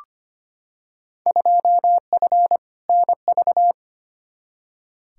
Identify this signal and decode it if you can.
Morse code, heard as a keyed tone, about 25 wpm, 700 Hz. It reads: 2F NV
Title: Field Day Exchanges - 25wpm